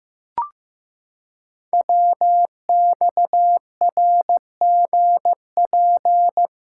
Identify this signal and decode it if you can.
Morse code, heard as a keyed tone, about 15 wpm, 700 Hz. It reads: WXRGP
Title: Code Groups - 5 Letters - 15wpm